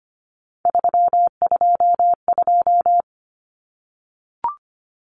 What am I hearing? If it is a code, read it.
Morse code, heard as a keyed tone, about 25 wpm, 700 Hz. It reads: 322
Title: Numbers - 3-Digit 25wpm